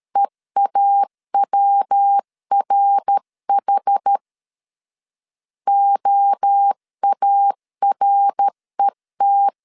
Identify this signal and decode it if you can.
Morse code, heard as a keyed tone, about 13 wpm, 800 Hz. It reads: EAWRH OARET
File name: CW / DET / 1 3 1500C S A 12wpm 2905